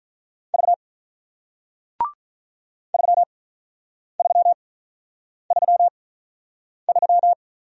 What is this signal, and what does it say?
Morse code, heard as a keyed tone, about 48 wpm, spaced as for 9 wpm, 700 Hz. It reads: V 3333